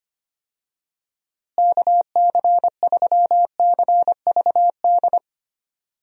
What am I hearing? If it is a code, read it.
Morse code, heard as a keyed tone, about 25 wpm, 700 Hz. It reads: KC3CVD
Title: USA Call Signs - 6 Characters - Repeated 3x - 25wpm